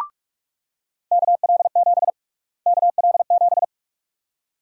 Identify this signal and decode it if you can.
Morse code, heard as a keyed tone, about 45 wpm, 700 Hz. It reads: KL7 KL7
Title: USA Call Sign Prefixes - Repeated 2x - 45wpm